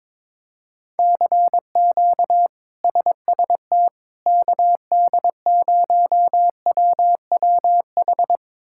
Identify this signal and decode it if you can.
Morse code, heard as a keyed tone, about 21 wpm, 700 Hz. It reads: CQ SST KD0WWH
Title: K1USN SST QSO Practice - 22wpm